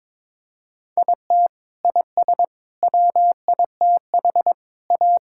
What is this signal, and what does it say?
Morse code, heard as a keyed tone, about 22 wpm, 700 Hz. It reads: IT IS WITH A
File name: practice / 22wpm / Sets of 7 Words - 22wpm